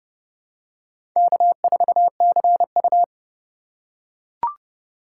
Morse code, heard as a keyed tone, about 30 wpm, 700 Hz. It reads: K4CU
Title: USA Call Signs - 4 Characters - 4x Character Spacing - 30wpm